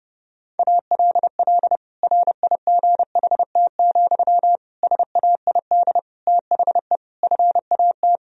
Morse code, heard as a keyed tone, about 32 wpm, 700 Hz. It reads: ALL RIGHT, SAID THE FAT
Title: Sentences from Top 1000 Words - Review Part 4 - 30wpm